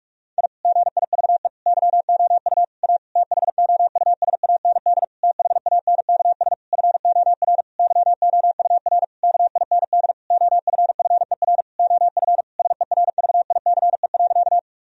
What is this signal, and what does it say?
Morse code, heard as a keyed tone, about 41 wpm, 700 Hz. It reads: I GIVE YOU A THOUSAND THANKS FOR YOUR KIND OFFER OF SERVICE.
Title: Sentences from Top 1000 Words - Review Part 4 - 45wpm